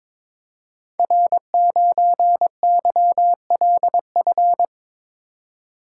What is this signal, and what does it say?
Morse code, heard as a keyed tone, about 22 wpm, 700 Hz. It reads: R9YLF
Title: Code Groups - 5 Letter:Numbers - 22wpm